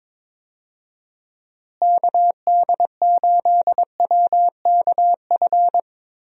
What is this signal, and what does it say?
Morse code, heard as a keyed tone, about 22 wpm, 700 Hz. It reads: KD8WKF
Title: USA Call Signs - 6 Characters - Repeated 3x - 22wpm